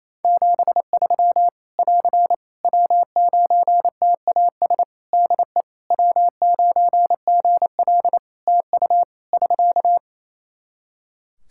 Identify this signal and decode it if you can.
Morse code, heard as a keyed tone, about 28 wpm, 700 Hz. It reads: 73 <AR> W9TAS DE W9GL TU <SK>
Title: Practice QSO - Exchange 4 - Ending - 28wpm